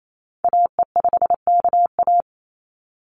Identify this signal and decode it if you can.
Morse code, heard as a keyed tone, about 28 wpm, 700 Hz. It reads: AE5KA
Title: NA SOTA Activator Callsigns 28wpm